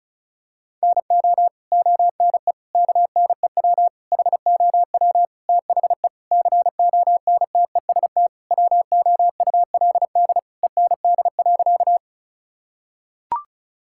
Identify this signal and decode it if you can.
Morse code, heard as a keyed tone, about 36 wpm, 700 Hz. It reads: NO ONE KNEW HOW THE CONTEST WOULD END.
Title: Sentences from Top 1600 Words 35wpm